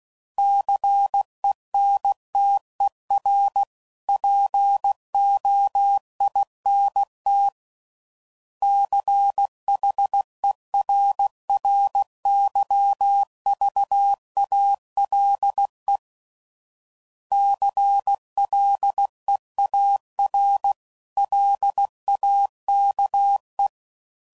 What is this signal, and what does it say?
Morse code, heard as a keyed tone, about 16 wpm, 800 Hz. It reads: CENTER POINT CHERRYVALE CLEAR LAKE
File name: CW Code Groups @15WPM